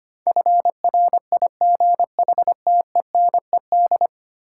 Morse code, heard as a keyed tone, about 25 wpm, 690 Hz. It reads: FRIGHTENED
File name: 10 Letter Words - First 3 Letters Given - 25wpm